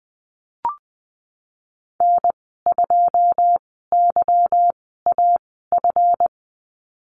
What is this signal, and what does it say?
Morse code, heard as a keyed tone, about 20 wpm, spaced as for 14 wpm, 700 Hz. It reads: N2YAF
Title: 161 Lesson 26g - Letter g - Call Signs 20wpm